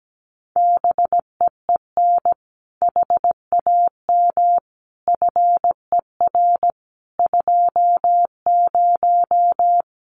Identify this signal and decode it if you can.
Morse code, heard as a keyed tone, about 17 wpm, 700 Hz. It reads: BEEN HAM FER 20
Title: Common QSO Phrases 17wpm